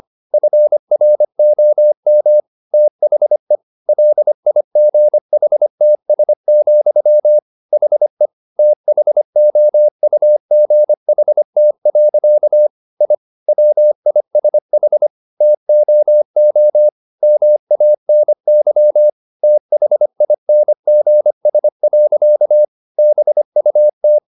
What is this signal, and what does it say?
Morse code, heard as a keyed tone, about 25 wpm, 600 Hz. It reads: FROM THE LIGHTS, HE THOUGHT. I WISH TOO MANY THINGS. BUT